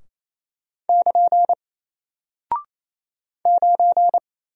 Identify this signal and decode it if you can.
Morse code, heard as a keyed tone, about 28 wpm, 700 Hz. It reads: <KN> 9
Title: Single Character (Letter:Number:Punctuation:Prosign) Alt 28wpm